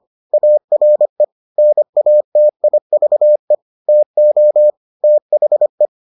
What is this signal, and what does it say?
Morse code, heard as a keyed tone, about 25 wpm, 600 Hz. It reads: ARE NATIVE TO THE